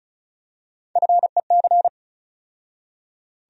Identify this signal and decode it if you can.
Morse code, heard as a keyed tone, about 36 wpm, 700 Hz. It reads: FEC
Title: USA Call Sign Postfixes - 3 Letters 35wpm